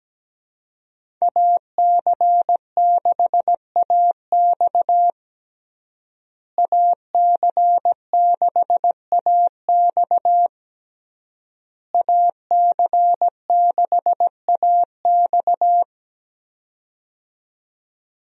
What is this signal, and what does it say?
Morse code, heard as a keyed tone, about 17 wpm, 700 Hz. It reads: AC6AX AC6AX AC6AX
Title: USA Call Signs - 5 Characters - Repeated 3x - 17wpm